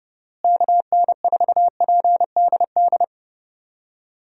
Morse code, heard as a keyed tone, about 30 wpm, 700 Hz. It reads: KN4PDD
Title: USA Call Signs - 6 Characters - Repeated 2x - 30wpm